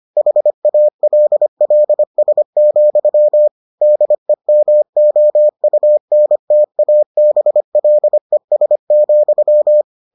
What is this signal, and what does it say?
Morse code, heard as a keyed tone, about 25 wpm, 600 Hz. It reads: HALLS, DEMOUNTABLES,